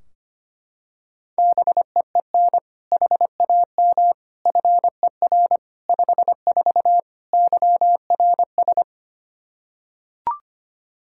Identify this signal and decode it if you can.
Morse code, heard as a keyed tone, about 25 wpm, 700 Hz. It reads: BEEN HAM FER 54 YRS
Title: Practice QSO - Exchange 3 - More Info - 25wpm